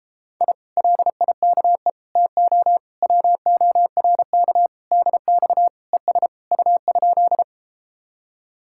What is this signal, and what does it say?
Morse code, heard as a keyed tone, about 33 wpm, 700 Hz. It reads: I LIKE TO WORK DX ES U?